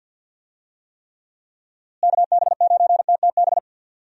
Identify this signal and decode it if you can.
Morse code, heard as a keyed tone, about 53 wpm, 700 Hz. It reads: KB9TTB
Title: USA Call Signs - 6 Characters - Repeated 2x - 50wpm